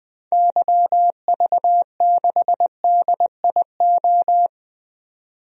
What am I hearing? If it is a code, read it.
Morse code, heard as a keyed tone, about 20 wpm, 700 Hz. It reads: YV6DIO
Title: DXCC-100 Call Signs - 6 Characters - 20wpm